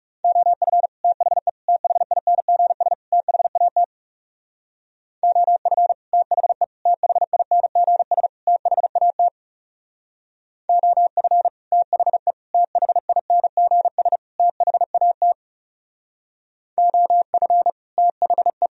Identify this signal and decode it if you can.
Morse code, heard as a keyed tone, about 37 wpm, 700 Hz. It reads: OF THE THINGS THAT OF THE THINGS THAT OF THE THINGS THAT OF THE
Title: Speed-Racing - Sets of 4 Words 30wpm